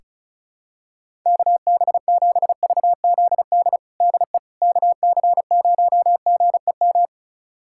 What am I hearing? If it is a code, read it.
Morse code, heard as a keyed tone, about 36 wpm, 700 Hz. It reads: KB7VZD DE KC0GEM
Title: USA Callsign DE Callsign 35wpm